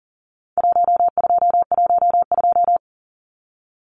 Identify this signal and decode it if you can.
Morse code, heard as a keyed tone, about 40 wpm, 700 Hz. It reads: 1212
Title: Numbers - Contest Style - 4-Digit 40wpm